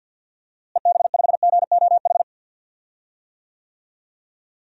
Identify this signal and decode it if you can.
Morse code, heard as a keyed tone, about 52 wpm, 700 Hz. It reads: EB5GOH
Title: DXCC-100 Call Signs - 6 Characters - Repeated 2x - 50wpm